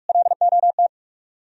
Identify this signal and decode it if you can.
Morse code, heard as a keyed tone, about 58 wpm, 700 Hz. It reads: LOT